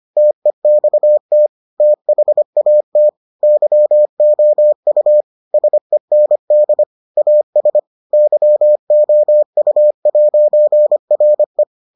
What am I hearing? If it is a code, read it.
Morse code, heard as a keyed tone, about 25 wpm, 600 Hz. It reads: TEXT THAT YOU SEND AS YOU'RE